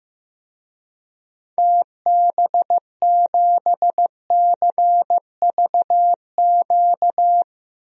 Morse code, heard as a keyed tone, about 15 wpm, 700 Hz. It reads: TB7CVQ
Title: DXCC-100 Call Signs - 6 Characters - 15wpm